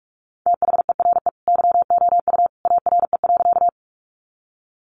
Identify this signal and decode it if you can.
Morse code, heard as a keyed tone, about 44 wpm, 700 Hz. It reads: THERE YOU ARE.
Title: Sentences from Top 100 Words 45wpm